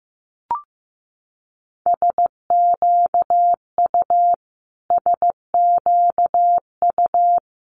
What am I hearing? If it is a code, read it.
Morse code, heard as a keyed tone, about 15 wpm, 700 Hz. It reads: SQU SQU
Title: Multi-letter Phonemes Repeated 3x 15wpm